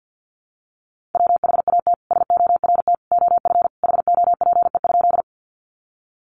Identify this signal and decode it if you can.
Morse code, heard as a keyed tone, about 50 wpm, 700 Hz. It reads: WHAT SORT OF HOPE?
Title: Sentences from Top 1300 Words 50wpm